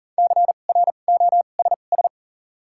Morse code, heard as a keyed tone, about 40 wpm, spaced as for 29 wpm, 700 Hz. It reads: CROSS
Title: 2x Character Spacing - Top 500 Words - 40wpm